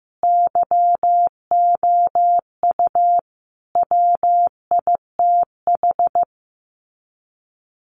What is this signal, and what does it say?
Morse code, heard as a keyed tone, about 15 wpm, 700 Hz. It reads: YOU WITH
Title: Sets of 5 Words Encore2 15wpm